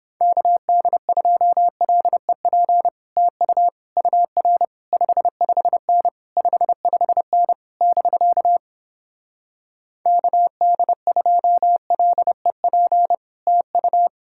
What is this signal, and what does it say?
Morse code, heard as a keyed tone, about 31 wpm, 700 Hz. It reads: KD2LEP TU UR 55N 55N <BK> KD2LEP TU